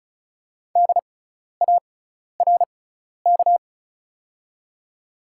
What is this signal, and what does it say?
Morse code, heard as a keyed tone, about 33 wpm, spaced as for 12 wpm, 690 Hz. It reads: DARK
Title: 6x Character Spacing - Top 500 Words - 35wpm